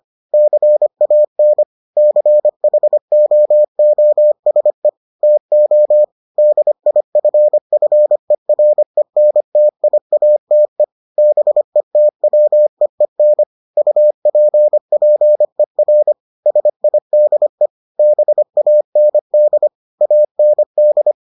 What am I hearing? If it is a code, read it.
Morse code, heard as a keyed tone, about 25 wpm, 600 Hz. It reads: CAN CHOOSE TO DIFFERENTIATE BETWEEN UPPER SIDE BAND AND